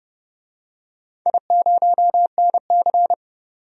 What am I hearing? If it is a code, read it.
Morse code, heard as a keyed tone, about 32 wpm, 700 Hz. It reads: I0NC
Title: DXCC-100 Call Signs - 4 Characters - 30wpm